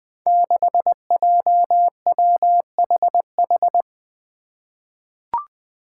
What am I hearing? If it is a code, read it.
Morse code, heard as a keyed tone, about 20 wpm, 700 Hz. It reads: 6JWHH
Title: Five Challenging Letter-Number Combinations 20wpm